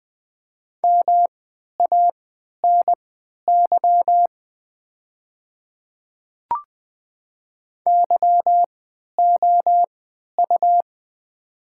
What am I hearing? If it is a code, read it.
Morse code, heard as a keyed tone, about 20 wpm, spaced as for 11 wpm, 700 Hz. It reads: MANY YOU
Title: 3x Character Spacing - Top 100 Words - 20wpm